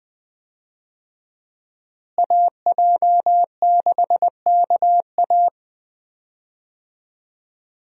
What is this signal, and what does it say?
Morse code, heard as a keyed tone, about 20 wpm, 700 Hz. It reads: AJ6KA
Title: USA Call Signs - 5 Characters - Repeated 2x - 20wpm